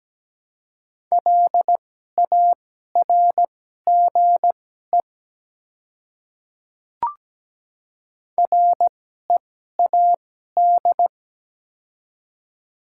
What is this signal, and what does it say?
Morse code, heard as a keyed tone, about 17 wpm, spaced as for 12 wpm, 700 Hz. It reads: LARGE READ
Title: 2x Character Spacing - Top 200 Words - 17wpm